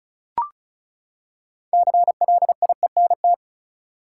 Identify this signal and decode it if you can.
Morse code, heard as a keyed tone, about 34 wpm, 700 Hz. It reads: CLIENT
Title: Six Letter Words - 35wpm